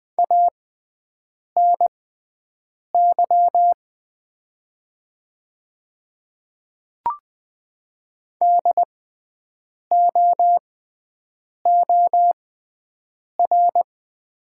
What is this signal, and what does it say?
Morse code, heard as a keyed tone, about 20 wpm, spaced as for 7 wpm, 700 Hz. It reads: ANY DOOR
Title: 6x Character Spacing - Top 300 Words - 20wpm